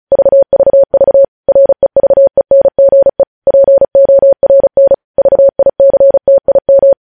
Morse code, heard as a keyed tone, about 36 wpm, 550 Hz. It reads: VVV REVENGE PORN VICTIM
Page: news headlines in morse code at 35 wpm